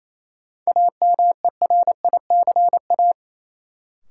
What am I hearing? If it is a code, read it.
Morse code, heard as a keyed tone, about 28 wpm, 700 Hz. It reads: AMERICA
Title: Sets of 5 Words - New Word Drill - 28wpm